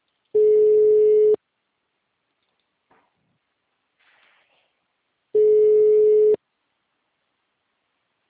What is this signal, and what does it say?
Ringback tone (Europe, 425 Hz), with speech.
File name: out- -2011- - - .35